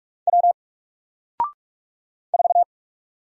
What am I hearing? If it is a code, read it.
Morse code, heard as a keyed tone, about 46 wpm, 700 Hz. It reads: W 4
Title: Single Letter-Number Alt 45wpm